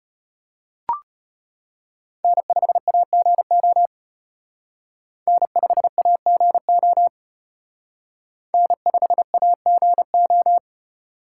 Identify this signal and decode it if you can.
Morse code, heard as a keyed tone, about 29 wpm, 700 Hz. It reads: N5AGO N5AGO N5AGO
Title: Speed Racing - USA 1x3 Call Signs 25wpm